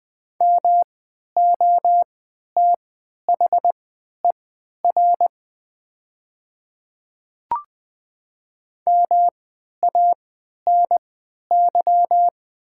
Morse code, heard as a keyed tone, about 20 wpm, spaced as for 11 wpm, 700 Hz. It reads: MOTHER MANY
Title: 3x Character Spacing - Top 500 Words - 20wpm